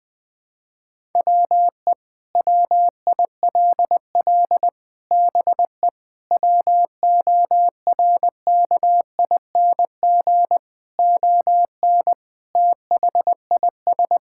Morse code, heard as a keyed tone, about 20 wpm, 700 Hz. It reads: WE WILL BE WORKING ON THIS
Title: Sentences with 6-Word Phrases - 20wpm